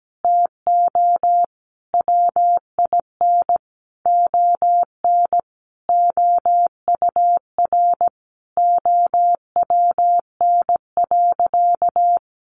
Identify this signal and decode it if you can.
Morse code, heard as a keyed tone, about 17 wpm, 700 Hz. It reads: TO WIN ON OUR OWN.
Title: Sentences from Top 1500 Words 17wpm